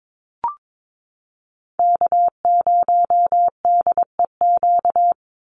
Morse code, heard as a keyed tone, about 22 wpm, 700 Hz. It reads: K0DEQ
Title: Call Signs Repeated 2x - 22wpm